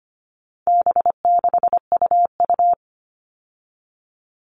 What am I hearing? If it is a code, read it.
Morse code, heard as a keyed tone, about 25 wpm, 700 Hz. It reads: B6UU
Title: All Cut Numbers - 4-Digit 25wpm